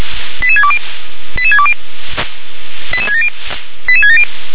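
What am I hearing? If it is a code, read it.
Five-tone selective call (ZVEI), heard as quick runs of five tones, tones of about 70 ms. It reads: 88620 88620 88670 88670
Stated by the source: ZVEI five-tone selective call 88775 received on SDR.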